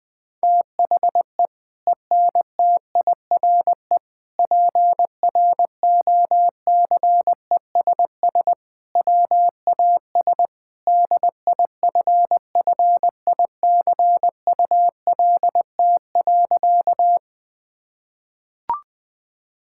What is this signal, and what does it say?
Morse code, heard as a keyed tone, about 20 wpm, 700 Hz. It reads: THE ENTIRE PROCESS WAS DIFFICULT.
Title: Sentences from Top 1000 Words - Review Part 5 - 20wpm